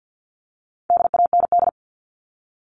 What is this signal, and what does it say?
Morse code, heard as a keyed tone, about 48 wpm, 700 Hz. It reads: DAND
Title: All Cut Numbers - 4-Digit 50wpm